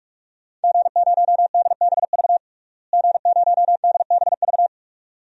Morse code, heard as a keyed tone, about 46 wpm, 700 Hz. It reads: G0DBV G0DBV
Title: DXCC-100 Call Signs - 5 Characters - Repeated 2x - 45wpm